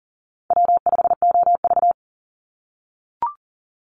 Morse code, heard as a keyed tone, about 40 wpm, 700 Hz. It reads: W5OV
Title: Call Signs - 40wpm